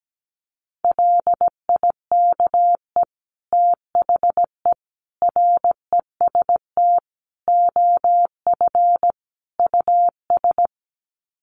Morse code, heard as a keyed tone, about 17 wpm, 700 Hz. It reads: LIKE THE REST OF US
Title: Sets of 5 Words Encore2 17wpm